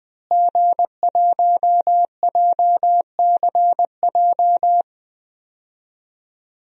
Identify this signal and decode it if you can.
Morse code, heard as a keyed tone, about 20 wpm, 700 Hz. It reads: G1JCJ
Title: DXCC-100 Call Signs - 5 Characters - Repeated 3x - 20wpm